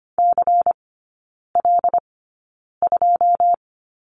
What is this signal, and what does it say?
Morse code, heard as a keyed tone, about 25 wpm, spaced as for 9 wpm, 700 Hz. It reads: CL2
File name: course / 25:10wpm / 093 Lesson 16e - Letter c - Three Characters 25wpm